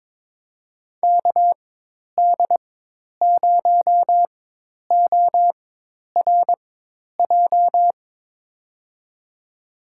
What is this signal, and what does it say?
Morse code, heard as a keyed tone, about 21 wpm, spaced as for 10 wpm, 700 Hz. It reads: KD0ORJ
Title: USA Call Signs - 6 Characters - 4x Character Spacing - 22wpm